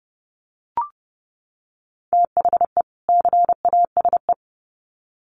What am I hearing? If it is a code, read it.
Morse code, heard as a keyed tone, about 28 wpm, 700 Hz. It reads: THE CASE
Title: Sets of 2 Words 30wpm